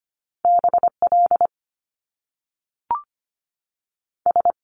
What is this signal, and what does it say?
Morse code, heard as a keyed tone, about 25 wpm, 700 Hz. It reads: BL S
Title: Top Two Letter Combinations in Words 25wpm